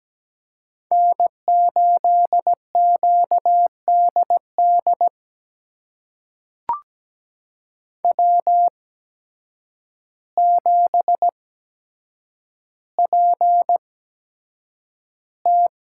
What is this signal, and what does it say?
Morse code, heard as a keyed tone, about 17 wpm, spaced as for 8 wpm, 700 Hz. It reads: N8QDD W7PT
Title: USA Call Signs - 8x Character Spacing - 17wpm